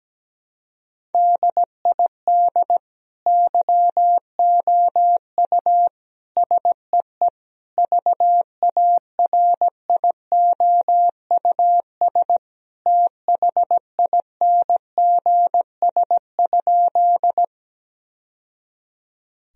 Morse code, heard as a keyed tone, about 17 wpm, 700 Hz. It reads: DID YOU SEE VARIOUS THINGS?
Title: Sentences from Top 1000 Words - Review Part 4 - 17wpm